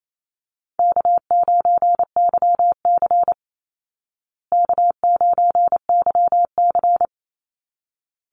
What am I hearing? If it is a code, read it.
Morse code, heard as a keyed tone, about 28 wpm, 700 Hz. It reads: K9YC K9YC
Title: Easy Call Signs Repeated 2x - 28wpm